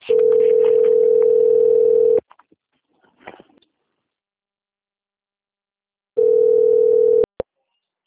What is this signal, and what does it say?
Ringback tone (North America), 440 + 480 Hz.
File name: out- - - - - .87257